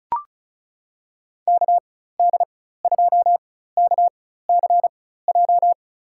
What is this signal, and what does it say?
Morse code, heard as a keyed tone, about 36 wpm, spaced as for 16 wpm, 700 Hz. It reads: KD2KCJ
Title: USA Call Signs - 4x Character Spacing - 35wpm